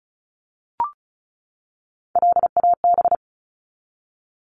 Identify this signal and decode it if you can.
Morse code, heard as a keyed tone, about 34 wpm, 700 Hz. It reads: LAB
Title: Top 1751-2000 Words 35wpm